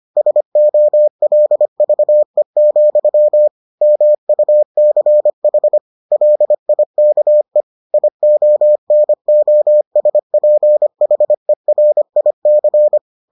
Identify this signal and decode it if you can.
Morse code, heard as a keyed tone, about 25 wpm, 600 Hz. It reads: SOLVE, MUCH LIKE IONOSPHERIC